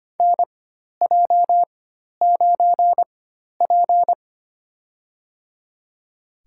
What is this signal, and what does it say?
Morse code, heard as a keyed tone, about 25 wpm, spaced as for 12 wpm, 700 Hz. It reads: NJ9P
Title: USA Call Signs - 4x Character Spacing - 25wpm